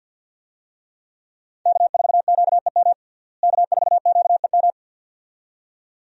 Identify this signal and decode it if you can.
Morse code, heard as a keyed tone, about 49 wpm, 700 Hz. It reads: K4QEM K4QEM
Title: USA Call Signs - 5 Characters - Repeated 2x - 50wpm